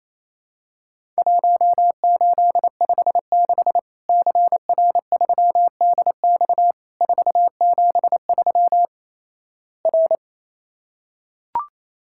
Morse code, heard as a keyed tone, about 28 wpm, 700 Hz. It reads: 1856 CR3DX 473 R
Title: Worked All Europe DX Contest - QTC Exchange - 28wpm